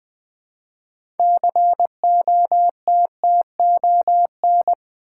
Morse code, heard as a keyed tone, about 20 wpm, 700 Hz. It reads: COTTON
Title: Build A Word - 2 Letters At A Time - 6-Letter Words - 20wpm